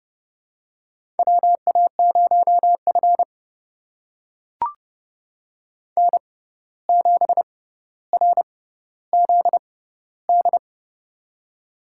Morse code, heard as a keyed tone, about 30 wpm, spaced as for 14 wpm, 700 Hz. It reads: WA0F N7RZD
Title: USA Call Signs - 6x Character Spacing - 30wpm